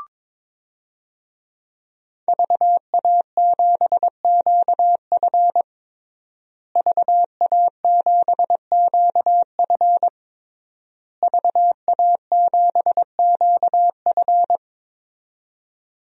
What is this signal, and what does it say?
Morse code, heard as a keyed tone, about 21 wpm, 700 Hz. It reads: VA7QF VA7QF VA7QF
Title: DXCC-100 Call Signs - 5 Characters - Repeated 3x - 22wpm